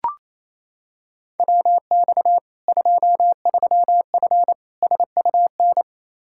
Morse code, heard as a keyed tone, about 28 wpm, 700 Hz. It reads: WX 23F SUN
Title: Practice QSO - Exchange 2 - Info - 28wpm